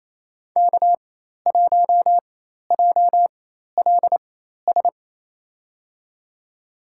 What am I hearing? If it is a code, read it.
Morse code, heard as a keyed tone, about 28 wpm, spaced as for 13 wpm, 700 Hz. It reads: K1JLS
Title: USA Call Signs - 5 Characters - 4x Character Spacing - 28wpm